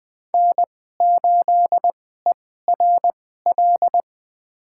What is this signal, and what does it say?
Morse code, heard as a keyed tone, about 20 wpm, spaced as for 15 wpm, 700 Hz. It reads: N8ERL
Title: USA Call Signs - 5 Characters - 2x Character Spacing - 20wpm